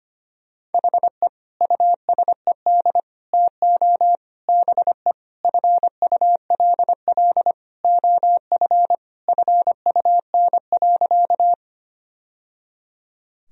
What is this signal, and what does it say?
Morse code, heard as a keyed tone, about 25 wpm, 700 Hz. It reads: HE USED TO BE FULL OF FUN.
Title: Sentences from Top 1000 Words - Review Part 4 - 25wpm